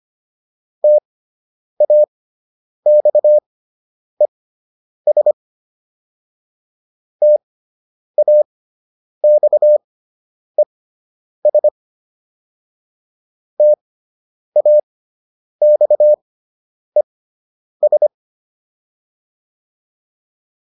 Morse code, heard as a keyed tone, about 25 wpm, spaced as for 9 wpm, 600 Hz. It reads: TAXES TAXES TAXES